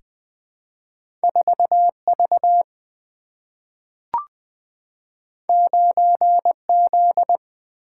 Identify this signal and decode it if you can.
Morse code, heard as a keyed tone, about 20 wpm, 690 Hz. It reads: 4V 9Z
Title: Two Letter-Number Combinations - 20wpm